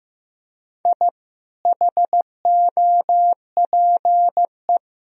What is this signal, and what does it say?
Morse code, heard as a keyed tone, about 15 wpm, 700 Hz. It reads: I HOPE
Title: Sentences from Top 1000 Words - Review Part 3 - 15wpm